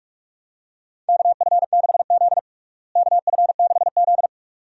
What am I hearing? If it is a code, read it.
Morse code, heard as a keyed tone, about 41 wpm, 700 Hz. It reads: KF6Z KF6Z
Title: USA Call Signs - 4 Characters - Repeated 2x - 45wpm